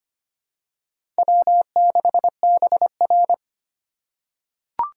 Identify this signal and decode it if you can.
Morse code, heard as a keyed tone, about 25 wpm, 700 Hz. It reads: W6BR
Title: USA Call Signs - 4 Characters - 2x Character Spacing - 25wpm